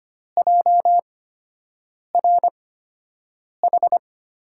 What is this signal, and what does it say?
Morse code, heard as a keyed tone, about 25 wpm, spaced as for 7 wpm, 700 Hz. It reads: JRH
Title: USA Call Signs - 5 Characters - 8x Character Spacing - 25wpm